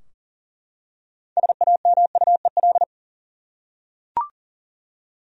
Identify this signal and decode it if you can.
Morse code, heard as a keyed tone, about 40 wpm, 700 Hz. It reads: SAMUEL
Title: USA Common Names - 40wpm